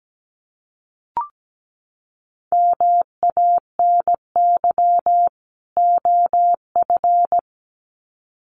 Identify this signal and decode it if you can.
Morse code, heard as a keyed tone, about 17 wpm, 700 Hz. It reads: MANY OF
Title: Sets of 2 Words 17wpm